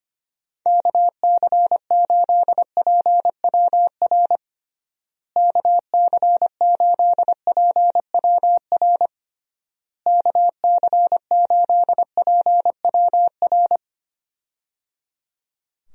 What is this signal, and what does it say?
Morse code, heard as a keyed tone, about 25 wpm, 700 Hz. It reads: KC8PWR KC8PWR KC8PWR
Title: USA Call Signs - 6 Characters - Repeated 3x - 25wpm